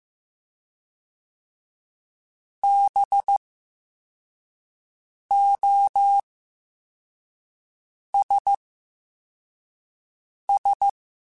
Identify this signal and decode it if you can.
Morse code, heard as a keyed tone, about 15 wpm, 800 Hz. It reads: B O S S
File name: Special INGs-mp3s / Special INGs-mp3s / SlowToFaster ing @10